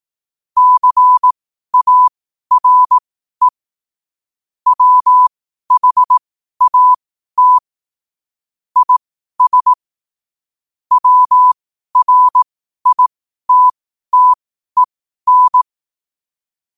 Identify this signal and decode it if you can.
Morse code, heard as a keyed tone, about 18 wpm, spaced as for 12 wpm, 1000 Hz. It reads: CARE WHAT IS WRITTEN
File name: qotdfiles / 12wpm